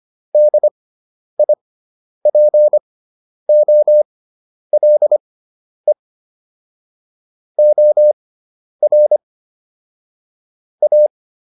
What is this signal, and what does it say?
Morse code, heard as a keyed tone, about 25 wpm, spaced as for 10 wpm, 600 Hz. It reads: DIPOLE OR A